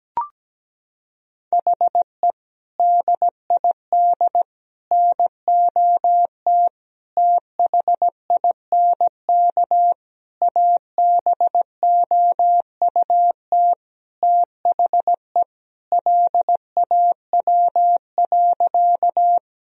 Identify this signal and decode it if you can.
Morse code, heard as a keyed tone, about 17 wpm, 700 Hz. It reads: HE DID NOT THINK ABOUT THE LAW.
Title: Sentences from Top 1000 Words - Review Part 3 - 17wpm